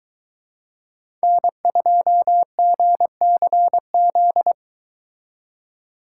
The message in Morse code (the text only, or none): N2GCZ